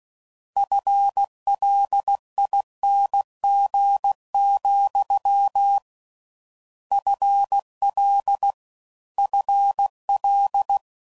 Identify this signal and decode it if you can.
Morse code, heard as a keyed tone, about 16 wpm, 800 Hz. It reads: FLING, FL FL